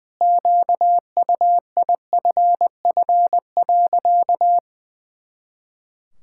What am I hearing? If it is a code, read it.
Morse code, heard as a keyed tone, about 21 wpm, 700 Hz. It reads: QUIFF.
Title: Pangram Challenge v2 - 20wpm